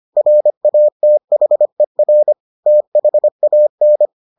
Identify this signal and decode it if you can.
Morse code, heard as a keyed tone, about 25 wpm, 600 Hz. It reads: RATHER THAN